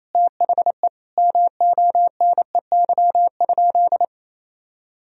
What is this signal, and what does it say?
Morse code, heard as a keyed tone, about 27 wpm, 700 Hz. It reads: THE MONEY?
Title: Sentences from Top 1000 Words - Review Part 2 - 28wpm